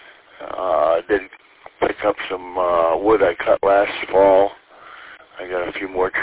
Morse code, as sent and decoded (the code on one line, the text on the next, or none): none